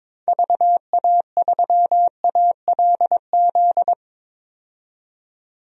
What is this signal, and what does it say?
Morse code, heard as a keyed tone, about 22 wpm, 700 Hz. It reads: VA3ALZ
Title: DXCC-100 Call Signs - 6 Characters - 22wpm